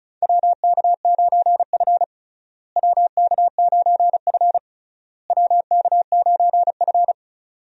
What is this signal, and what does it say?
Morse code, heard as a keyed tone, about 36 wpm, 700 Hz. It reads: WK9F WK9F WK9F
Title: USA Call Signs - 4 Characters - Repeated 3x - 35wpm